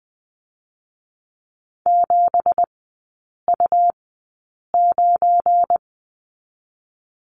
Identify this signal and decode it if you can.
Morse code, heard as a keyed tone, about 20 wpm, spaced as for 8 wpm, 700 Hz. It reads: 7U9
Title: 172 Lesson 28e - Number 9 - Three Characters 20wpm